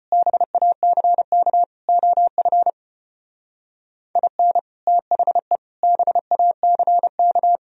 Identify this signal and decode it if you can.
Morse code, heard as a keyed tone, about 31 wpm, 700 Hz. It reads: BACK OF IN THE BACK
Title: Speed-Racing - Sets of 4 Words 25wpm